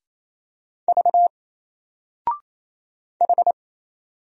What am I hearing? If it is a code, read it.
Morse code, heard as a keyed tone, about 28 wpm, 700 Hz. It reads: V H
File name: Single-Letter-Number-28wpm